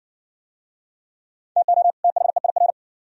Morse code, heard as a keyed tone, about 60 wpm, 700 Hz. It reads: TO THEIR